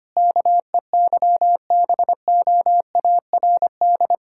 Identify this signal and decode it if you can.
Morse code, heard as a keyed tone, about 25 wpm, 690 Hz. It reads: KEYBOARD
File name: practice / 25wpm / Difficult Words with Hints - 25wpm